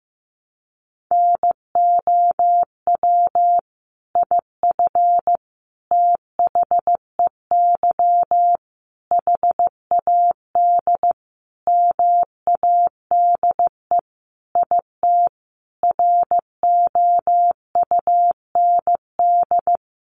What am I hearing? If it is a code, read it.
Morse code, heard as a keyed tone, about 14 wpm, 700 Hz. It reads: NOW IF THEY HAD MADE IT ROUND
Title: Sentences from Top 500 Words 15wpm